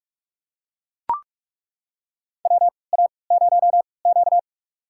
Morse code, heard as a keyed tone, about 41 wpm, spaced as for 25 wpm, 700 Hz. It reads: WA0Q
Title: USA Call Signs - 3x Character Spacing - 45wpm